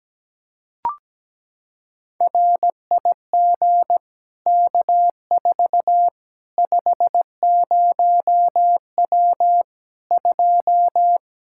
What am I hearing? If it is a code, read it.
Morse code, heard as a keyed tone, about 17 wpm, 700 Hz. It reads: RIG K4 50W 2
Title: Practice QSO - Exchange 2 - Info - 17wpm